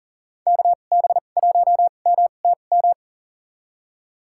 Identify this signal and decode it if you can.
Morse code, heard as a keyed tone, about 37 wpm, spaced as for 28 wpm, 700 Hz. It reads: KB1MTM
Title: USA Call Signs - 6 Characters - 2x Character Spacing - 40wpm